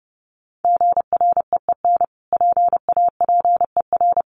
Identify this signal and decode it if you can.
Morse code, heard as a keyed tone, about 30 wpm, 700 Hz. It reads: GREEN PAPER
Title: Sentences from Top 1500 Words 30wpm